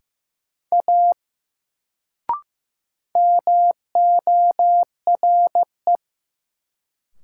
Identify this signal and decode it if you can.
Morse code, heard as a keyed tone, about 15 wpm, 700 Hz. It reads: A MORE
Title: Sentences from Top 100 Words - New Word Drill - 15wpm